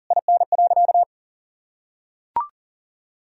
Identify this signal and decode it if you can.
Morse code, heard as a keyed tone, about 40 wpm, 700 Hz. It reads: IN.
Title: Sentences from Top 1000 Words - Review Part 2 - 40wpm